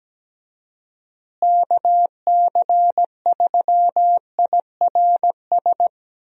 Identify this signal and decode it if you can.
Morse code, heard as a keyed tone, about 17 wpm, 700 Hz. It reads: KC3IRS
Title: USA Call Signs - 6 Characters - Repeated 3x - 17wpm